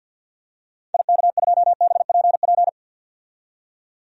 Morse code, heard as a keyed tone, about 50 wpm, 700 Hz. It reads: IK2BPP
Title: DXCC-100 Call Signs - 6 Characters - 50wpm